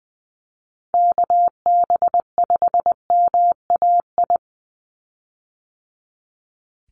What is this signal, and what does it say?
Morse code, heard as a keyed tone, about 20 wpm, 700 Hz. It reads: KB5MAI
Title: USA Call Signs 20wpm